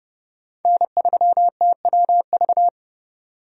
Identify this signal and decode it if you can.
Morse code, heard as a keyed tone, about 32 wpm, 700 Hz. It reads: N3TWV
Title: USA Call Signs - 5 Characters - 30wpm